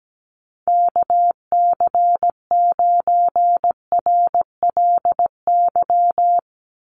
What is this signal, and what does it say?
Morse code, heard as a keyed tone, about 17 wpm, 700 Hz. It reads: KC9RLY
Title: USA Call Signs 17wpm